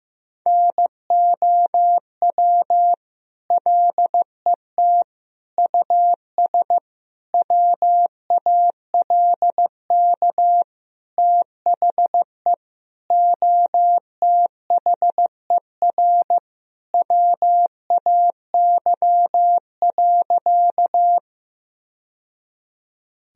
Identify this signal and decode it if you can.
Morse code, heard as a keyed tone, about 15 wpm, 700 Hz. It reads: NOW LET US WALK THE OTHER WAY.
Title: Sentences from Top 1000 Words - Review Part 1 - 15wpm